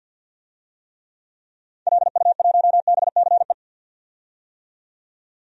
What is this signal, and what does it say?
Morse code, heard as a keyed tone, about 45 wpm, 700 Hz. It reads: LU1BCE
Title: DXCC-100 Call Signs - 6 Characters - 50wpm